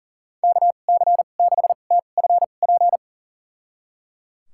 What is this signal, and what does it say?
Morse code, heard as a keyed tone, about 37 wpm, spaced as for 28 wpm, 690 Hz. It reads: KC6TFP
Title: USA Call Signs - 6 Characters - 2x Character Spacing - 40wpm